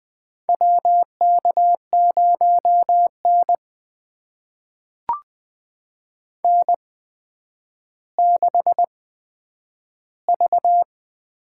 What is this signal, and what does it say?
Morse code, heard as a keyed tone, about 20 wpm, spaced as for 10 wpm, 700 Hz. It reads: WK0N N6V
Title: USA Call Signs - 8x Character Spacing - 20wpm